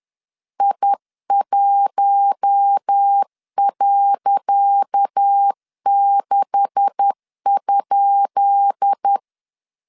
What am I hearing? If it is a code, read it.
Morse code, heard as a keyed tone, about 10 wpm, 800 Hz. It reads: I1.6?
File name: LEZIONI / 10 SIGNS / 10C DOT 10wpm 2940